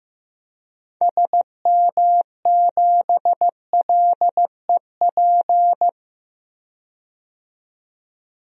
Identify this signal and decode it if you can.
Morse code, heard as a keyed tone, about 15 wpm, 700 Hz. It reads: SM7LEP
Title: DXCC-100 Call Signs - 6 Characters - 15wpm